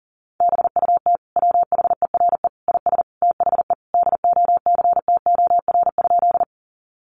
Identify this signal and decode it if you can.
Morse code, heard as a keyed tone, about 40 wpm, 700 Hz. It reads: BUT WHERE IS THE DOCTOR?
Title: Sentences from Top 1500 Words 40wpm